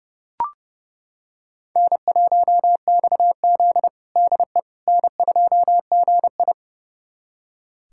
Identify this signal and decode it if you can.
Morse code, heard as a keyed tone, about 30 wpm, 700 Hz. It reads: N1XZ DE N2GI
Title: USA Callsign DE Callsign - 4 Characters - 30wpm